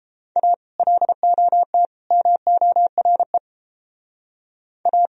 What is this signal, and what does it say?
Morse code, heard as a keyed tone, about 30 wpm, 700 Hz. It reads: A LOT MORE A